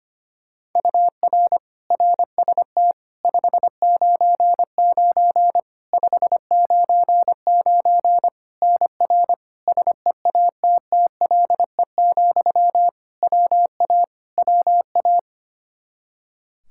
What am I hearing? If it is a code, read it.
Morse code, heard as a keyed tone, about 25 wpm, 700 Hz. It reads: UR RST 599 599 NR SEATTLE, WA WA
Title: Practice QSO - Exchange 1 - Beginning - 25wpm